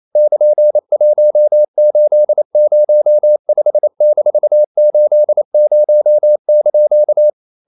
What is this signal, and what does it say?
Morse code, heard as a keyed tone, about 28 wpm, 600 Hz. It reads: <KN>1805-80)
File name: CW / Practice / EnglishQuotations0228